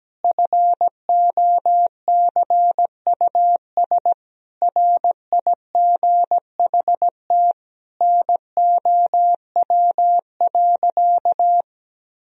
Morse code, heard as a keyed tone, about 17 wpm, 700 Hz. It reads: FOCUS RIGHT NOW.
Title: Sentences from Top 1000 Words - Review Part 4 - 17wpm